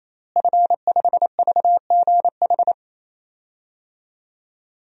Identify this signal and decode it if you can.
Morse code, heard as a keyed tone, about 28 wpm, 700 Hz. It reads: F5VGH
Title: DXCC-100 Call Signs - 5 Characters - Repeated 3x - 28wpm